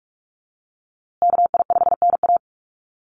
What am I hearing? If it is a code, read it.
Morse code, heard as a keyed tone, about 44 wpm, 700 Hz. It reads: KI5NA
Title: USA Call Signs Repeated 3x 45wpm